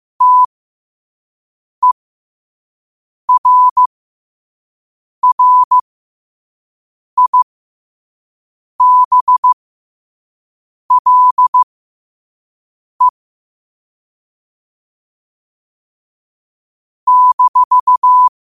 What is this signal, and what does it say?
Morse code, heard as a keyed tone, about 15 wpm, spaced as for 5 wpm, 1000 Hz. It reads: TERRIBLE -
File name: qotdfiles / 5wpm